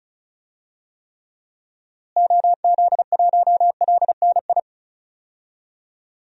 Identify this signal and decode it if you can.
Morse code, heard as a keyed tone, about 36 wpm, 700 Hz. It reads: OZ1LNI
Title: DXCC-100 Call Signs - 6 Characters - Repeated 3x - 35wpm